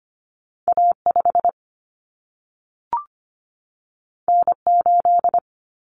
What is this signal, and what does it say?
Morse code, heard as a keyed tone, about 25 wpm, 700 Hz. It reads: A5 N8
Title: Common Cut Numbers - 2-Digit 25wpm